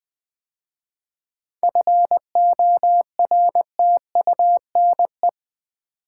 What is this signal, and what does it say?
Morse code, heard as a keyed tone, about 20 wpm, 700 Hz. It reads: FORTUNE
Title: 7 Letter Words - First 3 Letters Given - 20wpm